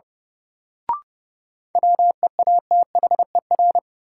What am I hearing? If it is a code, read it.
Morse code, heard as a keyed tone, about 30 wpm, 700 Hz. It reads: WEATHER
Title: Sentences from Top 800 Words - New Word Drill - 30wpm